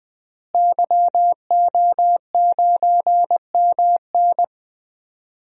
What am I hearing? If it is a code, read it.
Morse code, heard as a keyed tone, about 19 wpm, 700 Hz. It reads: YO9MN
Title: DXCC-100 Call Signs - 5 Characters - Repeated 3x - 20wpm